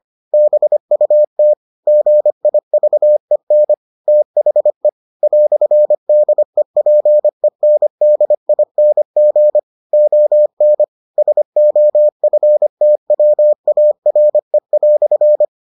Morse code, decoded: BUT GIVEN THE "DEPENDING ON SOFTWARE"

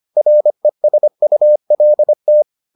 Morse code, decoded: RESULT

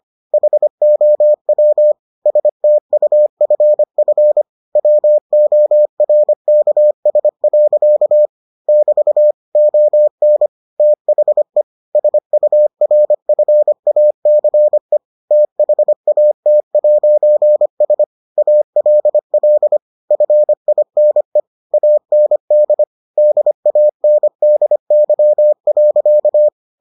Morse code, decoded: HOW STUFF WORKS. <BT> ON THE SURFACE THAT'S ALL FINE AND DANDY.

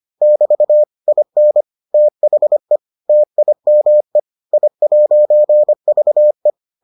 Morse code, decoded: <BT> IN THE TIME I'VE

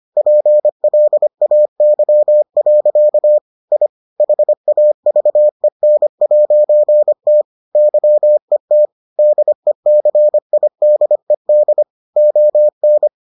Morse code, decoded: PLAY. I HAVEN'T YET DECIDED ON

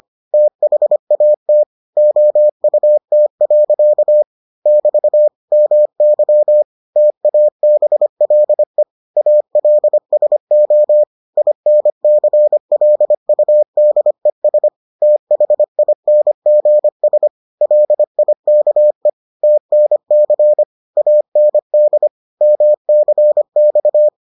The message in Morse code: THAT OUT. <BT> MY TABLE ALSO INCLUDES THINGS LIKE TNC AND MCX

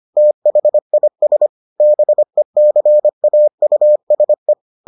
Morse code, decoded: THIS BECAUSE

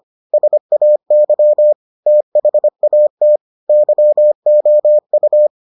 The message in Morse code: SAY THAT YOU

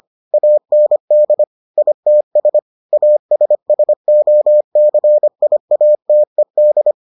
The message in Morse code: AND ITS ASSOCIATED

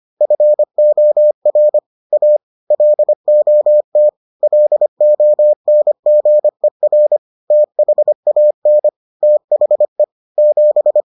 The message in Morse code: FOR A LOT LONGER THAN THE 7

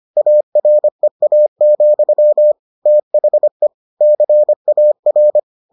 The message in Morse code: AREA, THE CAR

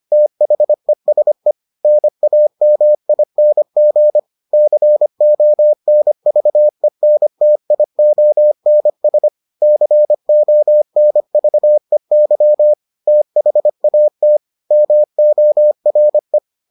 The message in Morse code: THESE NAMING CONVENTIONS CONVEY THAT MORE

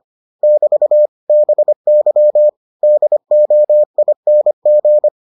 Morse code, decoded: <BT> BY DOING

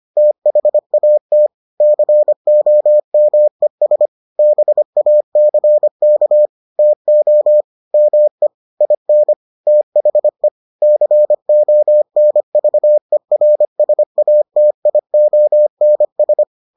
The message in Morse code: THAT COMES BACK TO ME IN THE CONVERSATIONS

